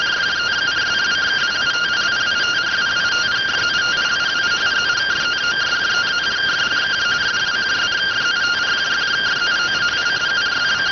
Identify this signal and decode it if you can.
Two-tone teleprinter signal, about 170 Hz shift.